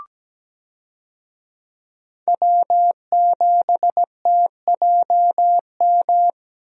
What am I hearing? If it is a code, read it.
Morse code, heard as a keyed tone, about 17 wpm, 700 Hz. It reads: W7TJM
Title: USA Call Signs - 5 Characters - Repeated 3x - 17wpm